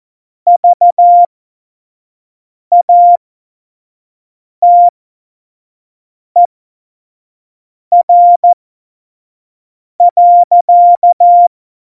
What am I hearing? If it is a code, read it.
Morse code, heard as a keyed tone, about 14 wpm, spaced as for 5 wpm, 700 Hz. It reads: VATER.
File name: Morsekurs / Witze / Witz 01. Dezember, 2020 25 F70